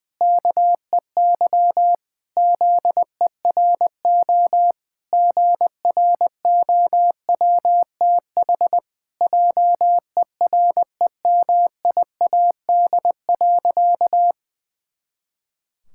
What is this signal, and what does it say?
Morse code, heard as a keyed tone, about 21 wpm, 700 Hz. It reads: KEY ZERO GROWTH JEREMIAD.
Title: Pangram Challenge v2 - 20wpm